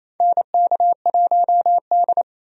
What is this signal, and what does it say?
Morse code, heard as a keyed tone, about 28 wpm, 700 Hz. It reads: NK1D
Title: USA Call Signs - 4 Characters - 28wpm